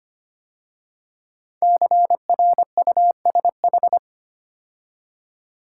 Morse code, heard as a keyed tone, about 25 wpm, 700 Hz. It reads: CRUSH
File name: 5 Letter Words - First 3 Letters Given - 25wpm